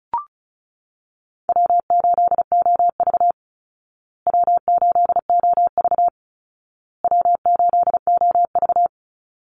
Morse code, heard as a keyed tone, about 34 wpm, 700 Hz. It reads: W8OV W8OV W8OV
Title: Easy Call Signs Repeated 3x - 35wpm